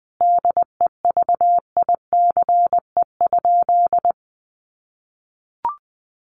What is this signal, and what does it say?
Morse code, heard as a keyed tone, about 20 wpm, 700 Hz. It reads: DEVICE?
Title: Sentences from Top 1500 Words 20wpm